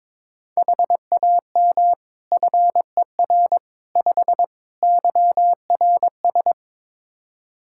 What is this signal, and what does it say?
Morse code, heard as a keyed tone, about 22 wpm, 700 Hz. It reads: HAM FER 5 YRS
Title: K1USN SST QSO Practice 22wpm